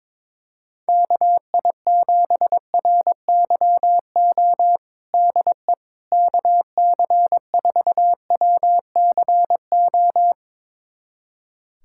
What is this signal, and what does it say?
Morse code, heard as a keyed tone, about 22 wpm, 700 Hz. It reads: KI7RYO DE KC4WCO
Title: USA Callsign DE Callsign - 6 Characters - 22wpm